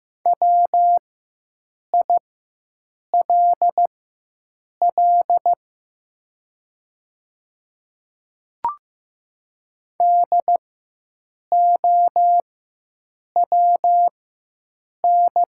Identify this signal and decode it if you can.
Morse code, heard as a keyed tone, about 15 wpm, spaced as for 7 wpm, 700 Hz. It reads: WILL DOWN
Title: 4x Character Spacing - Top 300 Words - 15wpm